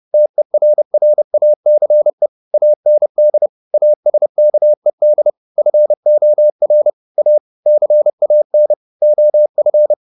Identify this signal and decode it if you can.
Morse code, heard as a keyed tone, about 30 wpm, 600 Hz. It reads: TERRACE AND ASKED FOR A CAN OF